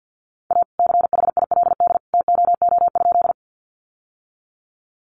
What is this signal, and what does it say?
Morse code, heard as a keyed tone, about 50 wpm, 700 Hz. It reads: A CHILD TOO?
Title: Sentences from Top 800 Words 50wpm